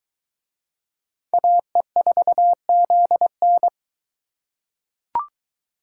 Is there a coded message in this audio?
Yes: Morse code from 1.3 to 3.7 s, heard as a keyed tone, about 20 wpm, 700 Hz. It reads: AE4ZN